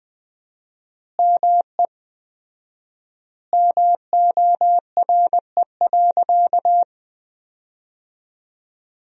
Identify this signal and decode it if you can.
Morse code, heard as a keyed tone, about 20 wpm, 700 Hz. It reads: ME MORE.
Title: 4x Word-Spacing - Sentences from Top 300 Words - 20wpm